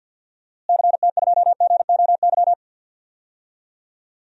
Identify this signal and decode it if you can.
Morse code, heard as a keyed tone, about 50 wpm, 700 Hz. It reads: CT2GOY
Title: DXCC-100 Call Signs - 6 Characters - Repeated 2x - 50wpm